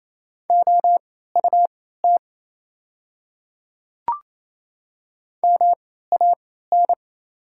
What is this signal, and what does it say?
Morse code, heard as a keyed tone, about 28 wpm, spaced as for 16 wpm, 700 Hz. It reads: OUT MAN
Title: 3x Character Spacing - Top 200 Words - 28wpm